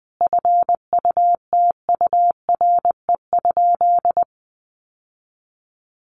Morse code, heard as a keyed tone, about 19 wpm, 700 Hz. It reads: FUTURE?
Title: Sentences from Top 700 Words 20wpm